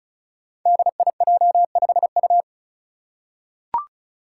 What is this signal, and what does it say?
Morse code, heard as a keyed tone, about 36 wpm, 700 Hz. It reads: DIJ5U
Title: Five Challenging Letter-Number Combinations 35wpm